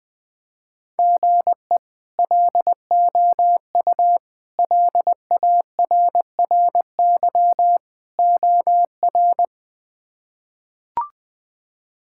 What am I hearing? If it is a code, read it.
Morse code, heard as a keyed tone, about 20 wpm, 700 Hz. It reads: GE LOU LARRY OR
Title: K1USN SST QSO Practice - 20wpm